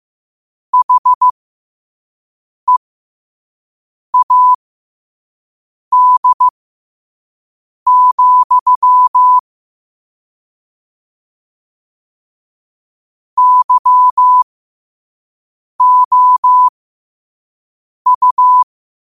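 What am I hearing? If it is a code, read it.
Morse code, heard as a keyed tone, about 15 wpm, spaced as for 5 wpm, 1000 Hz. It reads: HEAD, YOU
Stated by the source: Morse code quotes sent at 5 words per minute.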